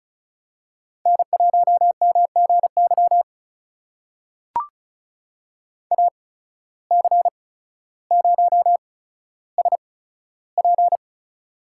Morse code, heard as a keyed tone, about 36 wpm, spaced as for 15 wpm, 700 Hz. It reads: N1MGY AC0SP
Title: USA Call Signs - 8x Character Spacing - 35wpm